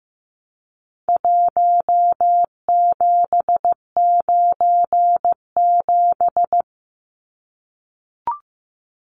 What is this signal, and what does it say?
Morse code, heard as a keyed tone, about 15 wpm, 700 Hz. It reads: 1797
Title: Numbers - Contest Style - 4-Digit 15wpm